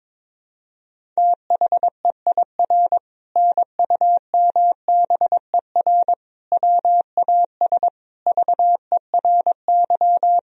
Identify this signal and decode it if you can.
Morse code, heard as a keyed tone, about 22 wpm, 700 Hz. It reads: THEIR NUMBER WAS VERY
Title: Sentences from Top 1000 Words - Review Part 1 - 22wpm